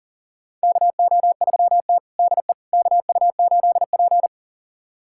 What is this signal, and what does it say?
Morse code, heard as a keyed tone, about 40 wpm, 700 Hz. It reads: KO3T DE KU8P
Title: USA Callsign DE Callsign - 4 Characters - 40wpm